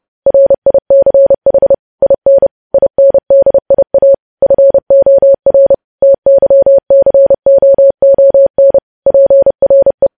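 Morse code, heard as a keyed tone, about 30 wpm, 550 Hz. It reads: RICH IN INDIA FOR TYCOON PRE
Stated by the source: Morse news headlines at 30 wpm.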